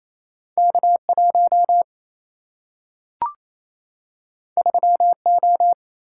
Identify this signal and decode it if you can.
Morse code, heard as a keyed tone, about 28 wpm, 700 Hz. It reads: K 1 3 O
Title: Single-Letter Single-Number 28wpm